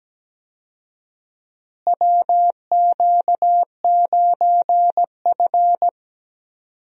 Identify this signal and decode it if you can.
Morse code, heard as a keyed tone, about 17 wpm, 700 Hz. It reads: WQ9F
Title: USA Call Signs - 4 Characters - Repeated 2x - 17wpm